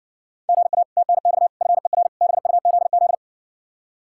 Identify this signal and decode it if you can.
Morse code, heard as a keyed tone, about 60 wpm, 700 Hz. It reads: GA TNX FER BUZZ